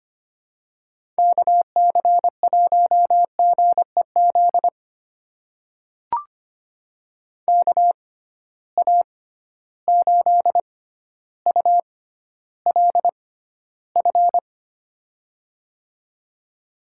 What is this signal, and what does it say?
Morse code, heard as a keyed tone, about 25 wpm, spaced as for 13 wpm, 700 Hz. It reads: KC1GEZ KA8ULF
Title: USA Call Signs - 6 Characters - 6x Character Spacing - 25wpm